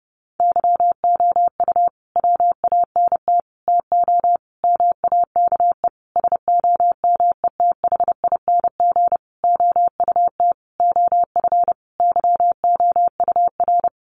Morse code, heard as a keyed tone, about 30 wpm, 700 Hz. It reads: YOU WANT TO MAKE SOMETHING OUT OF YOUR
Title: Sentences from Top 600 Words 30wpm